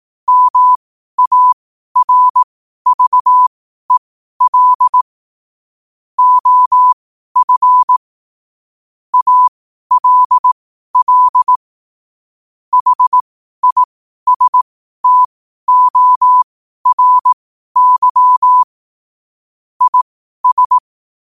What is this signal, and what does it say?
Morse code, heard as a keyed tone, about 18 wpm, spaced as for 12 wpm, 1000 Hz. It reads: MARVEL OF ALL HISTORY IS
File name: qotdfiles / 12wpm